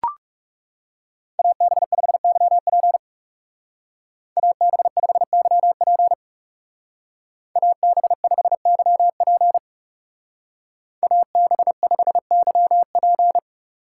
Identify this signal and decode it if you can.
Morse code, heard as a keyed tone, about 36 wpm, 700 Hz. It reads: AB5YP AB5YP AB5YP AB5YP
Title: Speed Racing - USA 2x2 Call Signs 30wpm